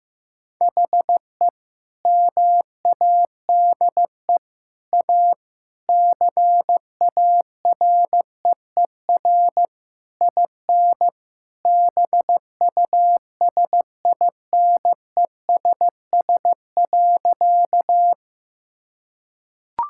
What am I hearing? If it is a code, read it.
Morse code, heard as a keyed tone, about 15 wpm, 700 Hz. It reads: HE MADE A CAREER IN BUSINESS.
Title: Sentences from Top 1000 Words - Review Part 6 - 15wpm